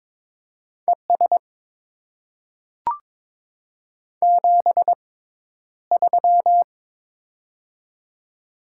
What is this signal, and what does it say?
Morse code, heard as a keyed tone, about 22 wpm, spaced as for 11 wpm, 690 Hz. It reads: ES 73
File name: Top 20 QSO Elements - 6x Character Spacing - 22wpm